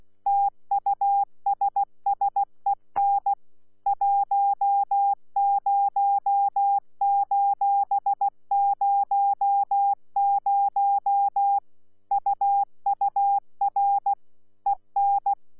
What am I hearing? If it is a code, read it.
Morse code, heard as a keyed tone, about 16 wpm, 800 Hz. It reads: TUSSEN 10:00 UUR EN
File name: Z PI4VRZ / PI4VRZ 14-jun-2025